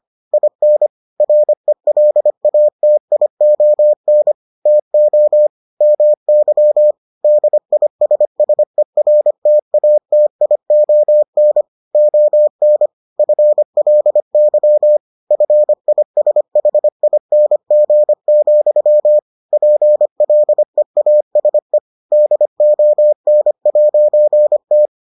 Morse code, heard as a keyed tone, about 25 wpm, 600 Hz. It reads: IN RELATION TO MY DISSERTATION ON FLY FISHING, PLEASE DON'T